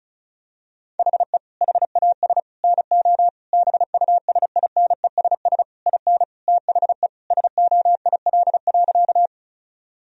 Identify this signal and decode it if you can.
Morse code, heard as a keyed tone, about 36 wpm, 700 Hz. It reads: HE HAS NO BUSINESS IN THE SOIL.
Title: Sentences from Top 1000 Words - Review Part 4 - 35wpm